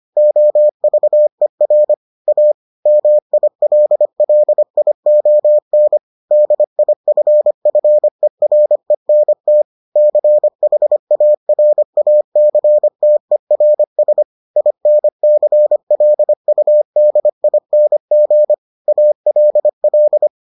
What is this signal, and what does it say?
Morse code, heard as a keyed tone, about 25 wpm, 600 Hz. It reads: OVER A MILLION DIFFERENT CHARACTERS INCLUDING ALL